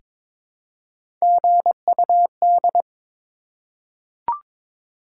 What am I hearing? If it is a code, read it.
Morse code, heard as a keyed tone, about 22 wpm, 700 Hz. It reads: GUD
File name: Top 20 QSO Elements - 22wpm